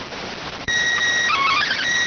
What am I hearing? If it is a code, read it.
SSTV (slow-scan television), PD 290.